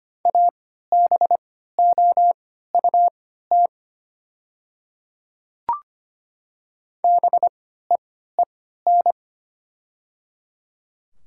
Morse code, heard as a keyed tone, about 25 wpm, spaced as for 14 wpm, 700 Hz. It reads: ABOUT BEEN
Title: 3x Character Spacing - Top 100 Words - 25wpm